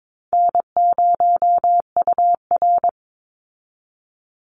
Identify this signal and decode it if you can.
Morse code, heard as a keyed tone, about 22 wpm, 700 Hz. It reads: N0UR
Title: Easy Call Signs Repeated 2x - 22wpm